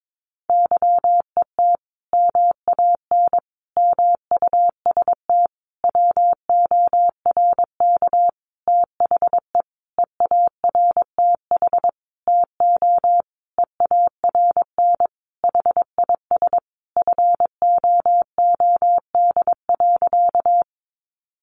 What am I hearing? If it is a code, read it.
Morse code, heard as a keyed tone, about 22 wpm, 700 Hz. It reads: YET MAN MUST WORK THE EARTH TO EARN HIS FOOD.
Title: Sentences from Top 1500 Words 22wpm